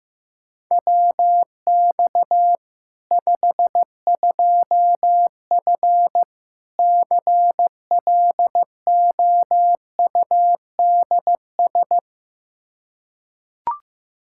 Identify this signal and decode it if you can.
Morse code, heard as a keyed tone, about 15 wpm, 700 Hz. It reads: WX 52F CLOUDS
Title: K1USN SST QSO Practice 15wpm